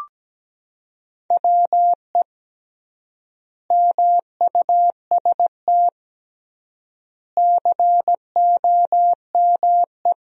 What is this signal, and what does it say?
Morse code, heard as a keyed tone, about 17 wpm, 700 Hz. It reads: WE MUST COME
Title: 3x Word-Spacing - Sentences from Top 200 Words - 17wpm